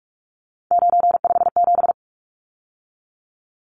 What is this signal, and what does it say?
Morse code, heard as a keyed tone, about 44 wpm, 700 Hz. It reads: 957
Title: Numbers - 3-Digit 45wpm